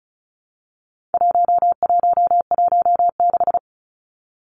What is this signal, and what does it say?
Morse code, heard as a keyed tone, about 34 wpm, 700 Hz. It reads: 1116
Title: Numbers - 4-Digit 35wpm